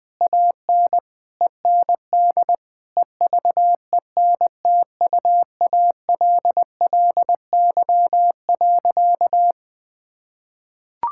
Morse code, heard as a keyed tone, about 20 wpm, 700 Hz. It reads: AN END EVENTUALLY.
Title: Sentences with 8-Word Phrases - 20wpm